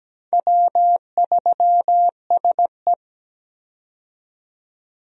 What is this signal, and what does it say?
Morse code, heard as a keyed tone, about 17 wpm, 700 Hz. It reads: W3SE
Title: USA Call Signs - 4 Characters - 17wpm